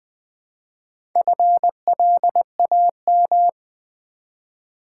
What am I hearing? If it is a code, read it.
Morse code, heard as a keyed tone, about 20 wpm, 690 Hz. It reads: FLAM
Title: Build A Word - 1 Letter At A Time - 5-Letter Words - 20wpm